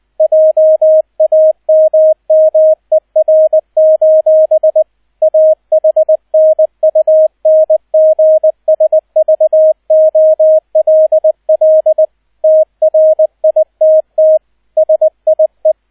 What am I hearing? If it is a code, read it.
Morse code, heard as a keyed tone, about 20 wpm, 630 Hz, with radio noise. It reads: JAMMER: AHNUNGSVOLL TRITT SIE
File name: Erster Streich 20wpm